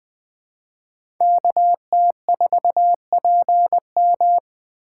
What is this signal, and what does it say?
Morse code, heard as a keyed tone, about 20 wpm, 700 Hz. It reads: KT4PM
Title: USA Call Signs - 2x Character Spacing - 20wpm